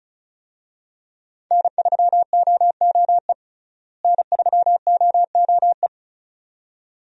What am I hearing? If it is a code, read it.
Morse code, heard as a keyed tone, about 34 wpm, 700 Hz. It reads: N3OOE N3OOE
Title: USA Call Signs - 5 Characters - Repeated 2x - 35wpm